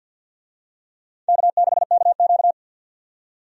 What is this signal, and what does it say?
Morse code, heard as a keyed tone, about 51 wpm, 700 Hz. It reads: K6KQ
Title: USA Call Signs - 4 Characters - 50wpm